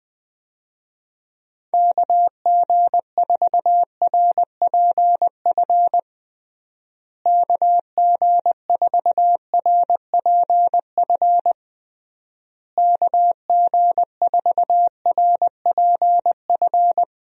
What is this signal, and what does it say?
Morse code, heard as a keyed tone, about 20 wpm, 700 Hz. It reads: KG4RPF KG4RPF KG4RPF
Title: USA Call Signs - 6 Characters - Repeated 3x - 20wpm